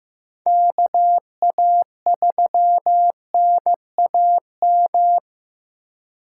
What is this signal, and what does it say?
Morse code, heard as a keyed tone, about 15 wpm, 700 Hz. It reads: KA3NAM
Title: K1USN SST QSO Practice - 15wpm